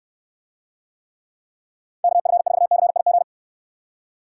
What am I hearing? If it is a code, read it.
Morse code, heard as a keyed tone, about 69 wpm, 700 Hz. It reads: KF4CIG